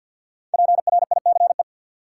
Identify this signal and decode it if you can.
Morse code, heard as a keyed tone, about 53 wpm, 700 Hz. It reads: PRICE
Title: Sentences from Top 600 Words - New Word Drill - 50wpm